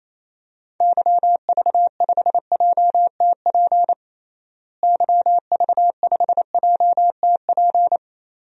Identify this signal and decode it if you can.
Morse code, heard as a keyed tone, about 28 wpm, 700 Hz. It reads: YV5JTP YV5JTP
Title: DXCC-100 Call Signs - 6 Characters - Repeated 3x - 28wpm